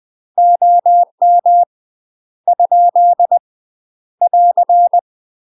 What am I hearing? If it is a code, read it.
Morse code, heard as a keyed tone, about 20 wpm, 700 Hz. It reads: OM ? <AR>
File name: UK MASTER QSOs TRAINING / 20WPM / UK MASTER QSOs 0073